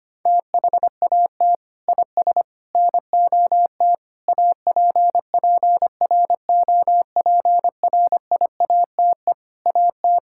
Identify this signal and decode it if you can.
Morse code, heard as a keyed tone, about 25 wpm, 700 Hz. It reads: THAT IS NOT APPROPRIATE AT
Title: Sentences from Top 1000 Words - Review Part 5 - 25wpm